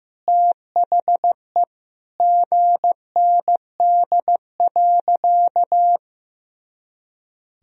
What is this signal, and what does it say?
Morse code, heard as a keyed tone, about 15 wpm, 700 Hz. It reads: THE GND.
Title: Sentences from Top 500 Words and Abbreviations 15wpm